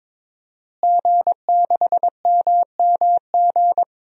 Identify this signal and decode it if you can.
Morse code, heard as a keyed tone, about 21 wpm, 700 Hz. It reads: G6MMG
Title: DXCC-100 Call Signs - 5 Characters - Repeated 3x - 22wpm